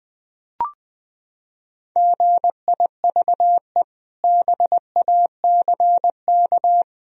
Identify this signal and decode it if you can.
Morse code, heard as a keyed tone, about 20 wpm, 700 Hz. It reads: GIVE BACK
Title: Sentences from Top 1000 Words - Review Part 1 - 20wpm